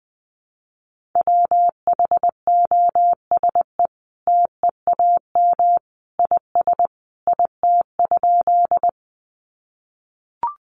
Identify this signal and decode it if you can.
Morse code, heard as a keyed tone, about 20 wpm, 700 Hz. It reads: WHOSE TEAM IS IT?
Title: Sentences from Top 1500 Words 20wpm